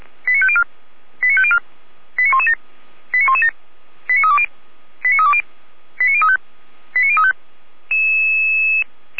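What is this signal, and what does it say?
Five-tone selective call (ZVEI), heard as quick runs of five tones, tones of about 70 ms. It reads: 89504 89504 89117 89117 89320 89320 89035 89035
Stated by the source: ZVEI five-tone call 89080 received on SDR.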